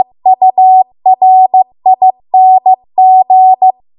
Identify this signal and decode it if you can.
Morse code, heard as a keyed tone, about 15 wpm, 750 Hz. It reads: URING
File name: Archive / 15 WPM / 15WPM